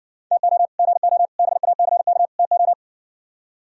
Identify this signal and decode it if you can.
Morse code, heard as a keyed tone, about 64 wpm, 700 Hz. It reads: TO GO BACK TO